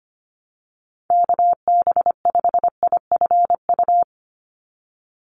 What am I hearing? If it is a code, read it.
Morse code, heard as a keyed tone, about 25 wpm, 700 Hz. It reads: KB5IFU
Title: USA Call Signs 25wpm